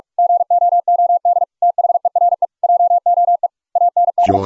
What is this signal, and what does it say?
Morse code, heard as a keyed tone, about 44 wpm, 700 Hz. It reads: GOOD THERE JOE AND